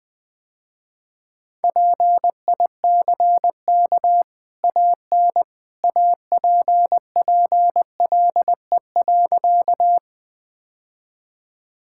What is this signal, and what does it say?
Morse code, heard as a keyed tone, about 20 wpm, 700 Hz. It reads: PICK AN APPLE.
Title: Easy Sentences with Hints - 20wpm